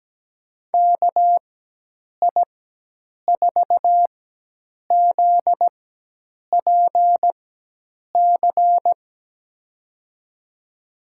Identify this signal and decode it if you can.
Morse code, heard as a keyed tone, about 17 wpm, spaced as for 8 wpm, 700 Hz. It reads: KI4ZPC
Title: USA Call Signs - 6 Characters - 4x Character Spacing - 17wpm